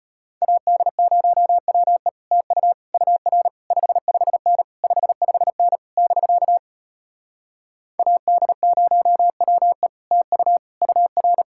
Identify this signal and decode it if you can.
Morse code, heard as a keyed tone, about 33 wpm, 700 Hz. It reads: AD0WE TU UR 55N 55N <BK> AD0WE TU UR